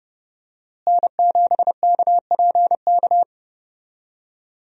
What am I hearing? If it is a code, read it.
Morse code, heard as a keyed tone, about 32 wpm, 700 Hz. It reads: N7KPK
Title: USA Call Signs - 5 Characters - Repeated 2x - 30wpm